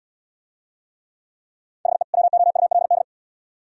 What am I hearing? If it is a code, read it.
Morse code, heard as a keyed tone, about 79 wpm, 700 Hz. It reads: HE WOULD